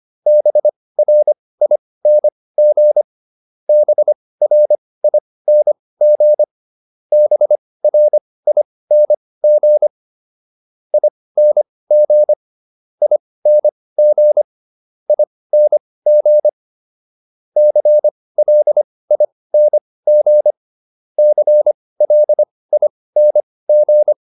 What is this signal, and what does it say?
Morse code, heard as a keyed tone, about 25 wpm, spaced as for 18 wpm, 600 Hz. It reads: BRING BRING BRING ING ING ING CLING CLING